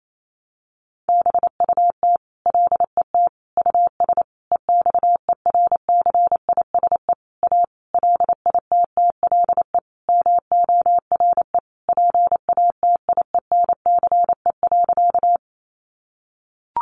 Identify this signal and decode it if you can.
Morse code, heard as a keyed tone, about 28 wpm, 700 Hz. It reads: BUT LET US EXERCISE A LITTLE MORE PATIENCE.
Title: Sentences from Top 1300 Words 28wpm